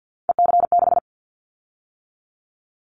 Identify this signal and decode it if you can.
Morse code, heard as a keyed tone, about 50 wpm, 700 Hz. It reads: EC6
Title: Callsign Prefixes for Top 50 DXCC - 50wpm